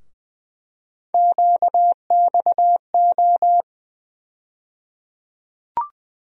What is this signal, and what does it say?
Morse code, heard as a keyed tone, about 20 wpm, 700 Hz. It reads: QXO
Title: USA Call Sign Postfixes - 3 Letters 20wpm